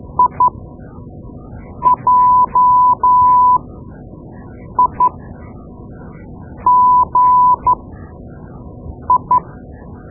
Morse code, heard as a keyed tone, about 12 wpm, spaced as for 5 wpm, 1000 Hz. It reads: IJIGI
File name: Morse Lessons / Morse Lessons / Tape-2B